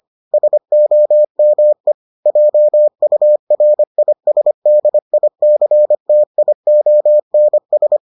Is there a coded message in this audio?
Yes: Morse code from 0.3 to 8.0 s, heard as a keyed tone, about 25 wpm, 600 Hz. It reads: SOME JURISDICTIONS